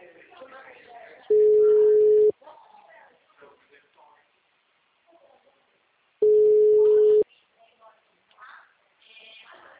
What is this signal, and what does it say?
Ringback tone (Europe, 425 Hz), with speech.